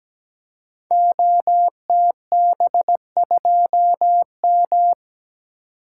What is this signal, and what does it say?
Morse code, heard as a keyed tone, about 17 wpm, 700 Hz. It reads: OTB2M
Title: Code Groups - 5 Letter:Numbers - 17wpm